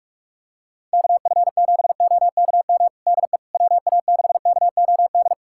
Morse code, heard as a keyed tone, about 46 wpm, 700 Hz. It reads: KF7OKM DE WA6KOD
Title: USA Callsign DE Callsign - 6 Characters - 45wpm